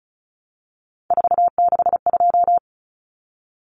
Morse code, heard as a keyed tone, about 34 wpm, 700 Hz. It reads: 462
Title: Common Cut Numbers - 3-Digit 35wpm